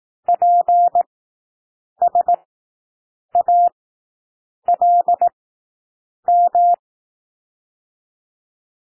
Morse code, heard as a keyed tone, about 18 wpm, spaced as for 7 wpm, 700 Hz. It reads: PSALM
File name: SOUND WORDS 18 wpm